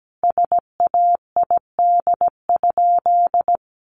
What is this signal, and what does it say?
Morse code, heard as a keyed tone, about 17 wpm, 700 Hz. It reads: SAID?
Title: Sentences from Top 900 Words 17wpm